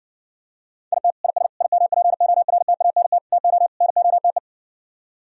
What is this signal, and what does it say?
Morse code, heard as a keyed tone, about 70 wpm, 700 Hz. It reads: IT IS IMPORTANT TO NOTE